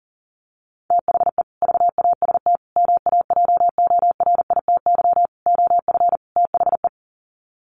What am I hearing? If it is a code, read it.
Morse code, heard as a keyed tone, about 40 wpm, 700 Hz. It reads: THE VAST MAJORITY OF THE
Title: Sets of 5 Words Encore 40wpm